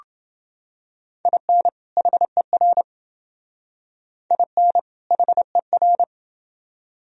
Morse code, heard as a keyed tone, about 29 wpm, 700 Hz. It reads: IN HER IN HER